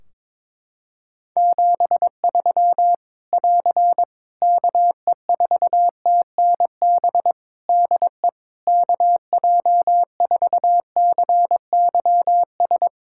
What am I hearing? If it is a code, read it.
Morse code, heard as a keyed tone, about 22 wpm, 700 Hz. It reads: 73 <AR> KE4TNB DE KJ4CYS
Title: Practice QSO - Exchange 4 - Ending - 22wpm